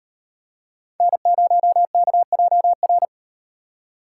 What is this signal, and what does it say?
Morse code, heard as a keyed tone, about 38 wpm, 700 Hz. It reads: N0KJR